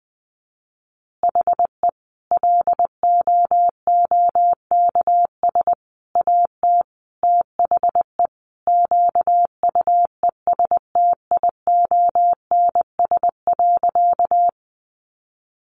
Morse code, heard as a keyed tone, about 20 wpm, 700 Hz. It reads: HE LOOKS AT THE QUESTIONS.
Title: Additional Easy Phrases 20wpm